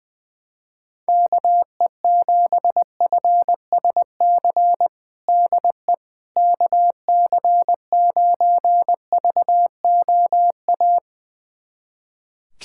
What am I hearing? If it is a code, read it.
Morse code, heard as a keyed tone, about 20 wpm, 700 Hz. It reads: KE7FSC DE KC9VOA
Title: USA Callsign DE Callsign - 6 Characters - 20wpm